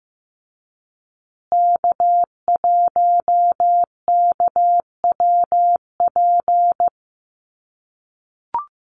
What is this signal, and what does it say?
Morse code, heard as a keyed tone, about 15 wpm, 700 Hz. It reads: K1KWP
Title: USA Call Signs Repeated 3x 15wpm